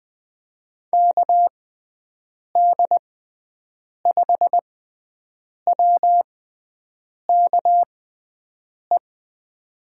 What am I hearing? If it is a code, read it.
Morse code, heard as a keyed tone, about 20 wpm, spaced as for 7 wpm, 700 Hz. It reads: KD5WKE
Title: USA Call Signs - 6 Characters - 6x Character Spacing - 20wpm